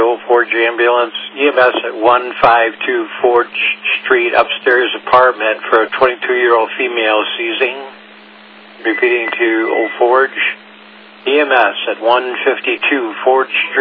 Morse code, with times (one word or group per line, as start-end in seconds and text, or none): none